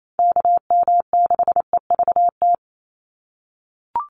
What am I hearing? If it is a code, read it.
Morse code, heard as a keyed tone, about 28 wpm, 700 Hz. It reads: KM6EVT
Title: USA Call Signs 28wpm